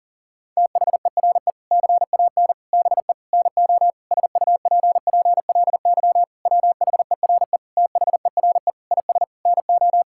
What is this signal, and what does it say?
Morse code, heard as a keyed tone, about 40 wpm, 700 Hz. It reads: THERE CAN BE NO SUPPLY WHERE THERE IS NO
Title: Sentences from Top 1000 Words - Review Part 5 - 40wpm